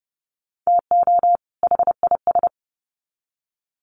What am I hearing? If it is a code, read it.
Morse code, heard as a keyed tone, about 30 wpm, 700 Hz. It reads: TO HIS
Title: Sets of 2 Words 30wpm